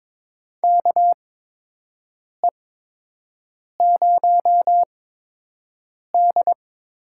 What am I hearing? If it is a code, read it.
Morse code, heard as a keyed tone, about 22 wpm, spaced as for 6 wpm, 700 Hz. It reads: KE0D
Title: USA Call Signs - 6 Characters - 8x Character Spacing - 22wpm